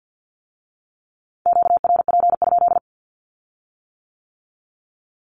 Morse code, heard as a keyed tone, about 50 wpm, 700 Hz. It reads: QRP?
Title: Q-Codes with Spoken Meaning - 50wpm